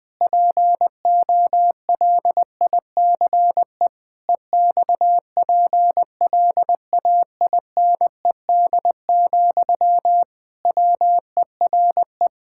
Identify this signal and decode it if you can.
Morse code, heard as a keyed tone, about 20 wpm, 700 Hz. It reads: POLICE EXPLAINED, WERE
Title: Pangram Challenge v2 - 20wpm